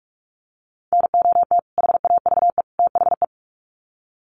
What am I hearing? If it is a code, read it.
Morse code, heard as a keyed tone, about 44 wpm, 700 Hz. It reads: NOT HAVE THE
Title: Sets of 3 Words 45wpm